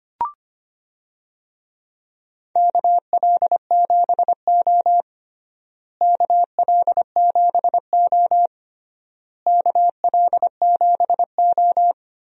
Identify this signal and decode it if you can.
Morse code, heard as a keyed tone, about 25 wpm, 700 Hz. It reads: KL7O KL7O KL7O
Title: DXCC-100 Call Signs - 4 Characters - Repeated 3x - 25wpm